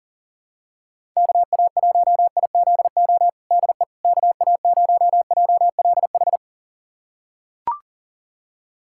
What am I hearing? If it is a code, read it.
Morse code, heard as a keyed tone, about 37 wpm, 700 Hz. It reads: KA1IZO DE KA0JLH
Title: USA Callsign DE Callsign - 6 Characters - 40wpm